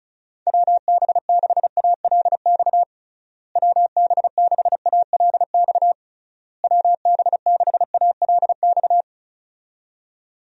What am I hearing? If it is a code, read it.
Morse code, heard as a keyed tone, about 36 wpm, 700 Hz. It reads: WB6ALX WB6ALX WB6ALX
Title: USA Call Signs - 6 Characters - Repeated 3x - 35wpm